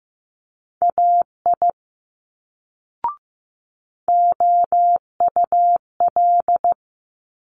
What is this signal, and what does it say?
Morse code, heard as a keyed tone, about 15 wpm, 700 Hz. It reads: AI OUL
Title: Multi-letter Phonemes Quick Fire 15wpm